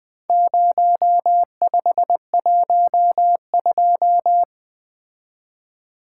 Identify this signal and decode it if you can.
Morse code, heard as a keyed tone, about 21 wpm, 700 Hz. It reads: 0512
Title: Four Letter-Number Combinations - 20wpm